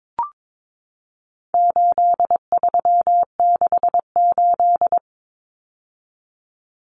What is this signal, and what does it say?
Morse code, heard as a keyed tone, about 22 wpm, 700 Hz. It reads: 8368
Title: Numbers - 4-Digit 22wpm